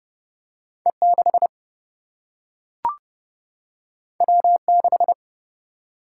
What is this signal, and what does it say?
Morse code, heard as a keyed tone, about 30 wpm, 700 Hz. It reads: E6 W6
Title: Single-Letter Single-Number 30wpm